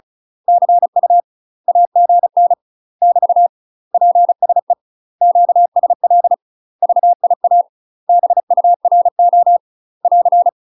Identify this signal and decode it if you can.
Morse code, heard as a keyed tone, about 35 wpm, 700 Hz. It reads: CU AGN <BT> PSE QSL VIA BURO <AR>
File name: UK MASTER QSOs TRAINING / 35WPM / UK MASTER QSOs 0005